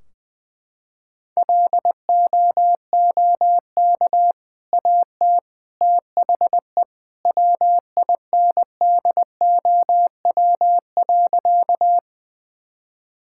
Morse code, heard as a keyed tone, about 21 wpm, 700 Hz. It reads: LOOK AT THE WINDOW.
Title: Sentences from Top 1600 Words 20wpm